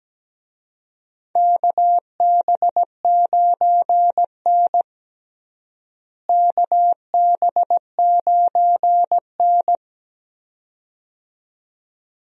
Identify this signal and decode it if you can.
Morse code, heard as a keyed tone, about 17 wpm, 700 Hz. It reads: KB9N KB9N
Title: USA Call Signs - 4 Characters - Repeated 2x - 17wpm